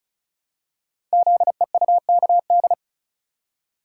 Morse code, heard as a keyed tone, about 36 wpm, 700 Hz. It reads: ZEUKD
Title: Code Groups - 5 Letters - 35wpm